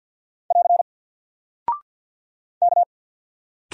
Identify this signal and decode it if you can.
Morse code, heard as a keyed tone, about 48 wpm, 700 Hz. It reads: <AR> K
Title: Single Character (Letter:Number:Punctuation:Prosign) Alt 50wpm